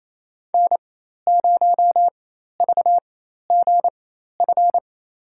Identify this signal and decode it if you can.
Morse code, heard as a keyed tone, about 28 wpm, spaced as for 13 wpm, 700 Hz. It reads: N0VGF
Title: USA Call Signs - 4x Character Spacing - 28wpm